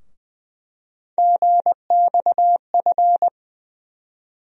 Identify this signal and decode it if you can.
Morse code, heard as a keyed tone, about 20 wpm, 700 Hz. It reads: GXF
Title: USA Call Sign Postfixes - 3 Letters 20wpm